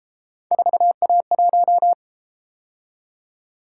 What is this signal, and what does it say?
Morse code, heard as a keyed tone, about 31 wpm, 700 Hz. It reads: 4A1